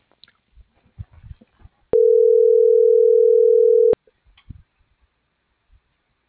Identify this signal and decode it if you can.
Ringback tone (North America), 440 + 480 Hz.